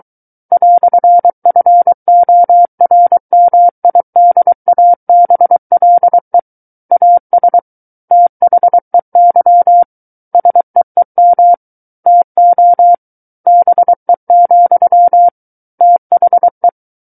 Morse code, heard as a keyed tone, about 23 wpm, 700 Hz. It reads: "FORMIDABLE AS THEY SEEM TO BE, THE
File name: Book One / 14-Chapter 14 - 23 WPM